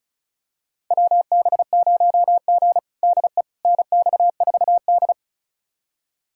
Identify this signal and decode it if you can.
Morse code, heard as a keyed tone, about 36 wpm, 700 Hz. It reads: WB0G DE NX4D
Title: USA Callsign DE Callsign - 4 Characters - 35wpm